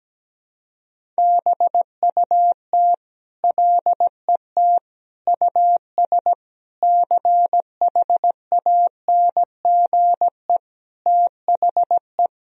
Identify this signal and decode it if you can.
Morse code, heard as a keyed tone, about 17 wpm, 700 Hz. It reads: BUT LET US CHANGE THE
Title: Sentences from Top 1000 Words - Review Part 1 - 17wpm